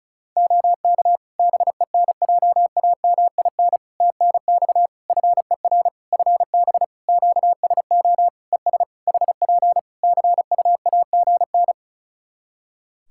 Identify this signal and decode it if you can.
Morse code, heard as a keyed tone, about 36 wpm, 700 Hz. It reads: OK BENJAMIN TNX FER FB QSO ES HP CUAGN
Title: Practice QSO - Exchange 4 - Ending - 35wpm